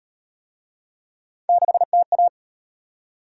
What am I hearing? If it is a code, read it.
Morse code, heard as a keyed tone, about 36 wpm, 700 Hz. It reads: 6TA